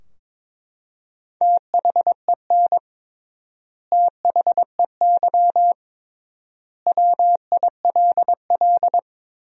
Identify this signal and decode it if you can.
Morse code, heard as a keyed tone, about 22 wpm, 700 Hz. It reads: THEN THEY WILL
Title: 3x Word-Spacing - Sentences from Top 200 Words - 22wpm